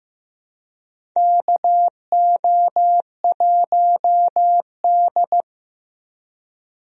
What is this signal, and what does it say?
Morse code, heard as a keyed tone, about 15 wpm, 700 Hz. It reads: KO1D
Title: USA Call Signs - 4 Characters - Repeated 3x - 15wpm